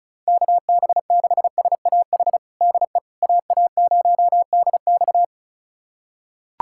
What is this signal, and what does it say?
Morse code, heard as a keyed tone, about 36 wpm, 700 Hz. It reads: KB6SAH DE AA0DX
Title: USA Callsign DE Callsign 35wpm